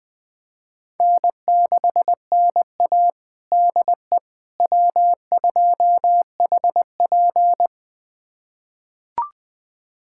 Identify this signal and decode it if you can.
Morse code, heard as a keyed tone, about 20 wpm, 700 Hz. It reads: N6NA DE W2HP
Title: USA Callsign DE Callsign - 4 Characters - 20wpm